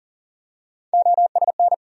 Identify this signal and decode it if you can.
Morse code, heard as a keyed tone, about 40 wpm, 700 Hz. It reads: OSN
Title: USA Call Sign Postfixes - 3 Letters 40wpm